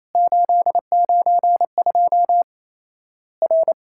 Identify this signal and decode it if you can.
Morse code, heard as a keyed tone, about 28 wpm, 700 Hz. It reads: 892 R
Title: Worked All Europe DX Contest - QTC Exchange - 28wpm